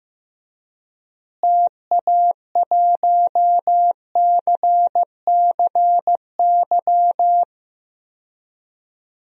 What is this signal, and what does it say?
Morse code, heard as a keyed tone, about 15 wpm, 700 Hz. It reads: TA1CCY
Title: DXCC-100 Call Signs - 6 Characters - Repeated 2x - 15wpm